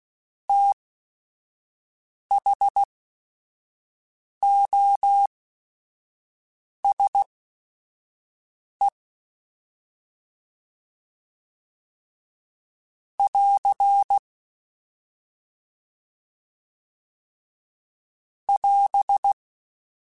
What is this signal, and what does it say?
Morse code, heard as a keyed tone, about 16 wpm, spaced as for 5 wpm, 800 Hz. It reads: THOSE <AR> <AS>
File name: T then H Runs / T then H Runs.mp3s / T then H Runs @5